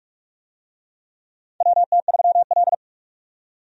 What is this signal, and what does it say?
Morse code, heard as a keyed tone, about 44 wpm, 700 Hz. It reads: WT3L
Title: USA Call Signs - 4 Characters - 45wpm